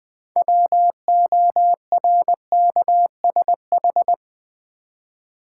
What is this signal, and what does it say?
Morse code, heard as a keyed tone, about 20 wpm, 700 Hz. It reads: WORKSH
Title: Build A Word - 2 Letters At A Time - 8-Letter Words - 20wpm